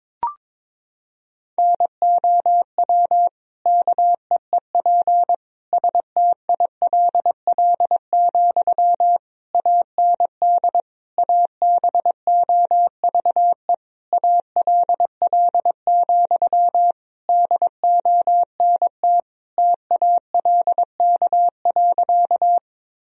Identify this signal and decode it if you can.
Morse code, heard as a keyed tone, about 22 wpm, 700 Hz. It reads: NOW KEEP STILL, AND ABOVE ALL, DONT TALK.
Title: Sentences from Top 1000 Words - Review Part 2 - 22wpm